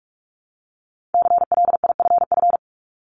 Morse code, heard as a keyed tone, about 46 wpm, 700 Hz. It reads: CLIFF
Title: Multi-letter Phonemes Repeated 3x 45wpm